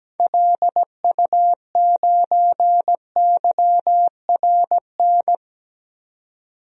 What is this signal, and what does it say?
Morse code, heard as a keyed tone, about 17 wpm, 700 Hz. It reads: LU9YRN
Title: DXCC-100 Call Signs - 6 Characters - 17wpm